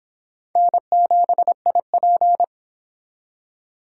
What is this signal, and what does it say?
Morse code, heard as a keyed tone, about 25 wpm, 700 Hz. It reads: N7IP